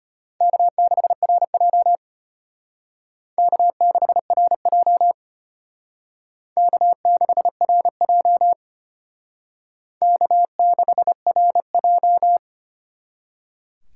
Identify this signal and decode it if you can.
Morse code, heard as a keyed tone, about 32 wpm, 700 Hz. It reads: K6RJ K6RJ K6RJ K6RJ